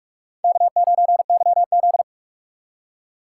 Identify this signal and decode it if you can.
Morse code, heard as a keyed tone, about 42 wpm, 700 Hz. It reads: K9YZ
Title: USA Call Signs - 4 Characters - 3x Character Spacing - 45wpm